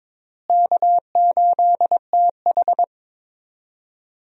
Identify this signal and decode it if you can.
Morse code, heard as a keyed tone, about 21 wpm, 700 Hz. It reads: K8TH
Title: USA Call Signs - 4 Characters - Repeated 3x - 22wpm